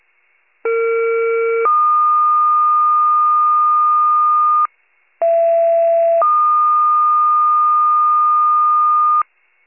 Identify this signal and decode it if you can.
Two-tone page, heard as a one-second tone, then a three-second tone, about 460 then 1220 Hz, about 670 then 1220 Hz. It is heard from 0.6 to 9.2 s.
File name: pager / 83 RESCUE DUTY / wcalert